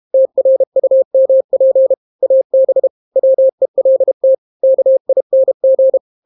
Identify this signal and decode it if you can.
Morse code, heard as a keyed tone, about 32 wpm, 530 Hz. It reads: TRUMP AB WELT KING